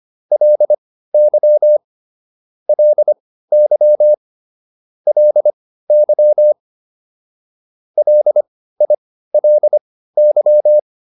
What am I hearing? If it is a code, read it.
Morse code, heard as a keyed tone, about 25 wpm, spaced as for 15 wpm, 600 Hz. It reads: LY LY LY LILY
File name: LY4 15